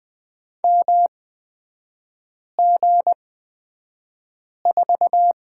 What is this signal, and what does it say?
Morse code, heard as a keyed tone, about 20 wpm, spaced as for 5 wpm, 690 Hz. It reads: MG4
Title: 229 Lesson 37e - Letter x - Three Characters 20wpm